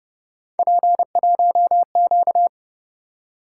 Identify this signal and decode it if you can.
Morse code, heard as a keyed tone, about 30 wpm, 700 Hz. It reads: P1Q